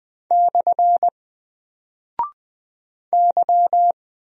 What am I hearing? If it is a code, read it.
Morse code, heard as a keyed tone, about 20 wpm, 700 Hz. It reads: / Y
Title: Single Character (Letter:Number:Punctuation:Prosign) Alt 20wpm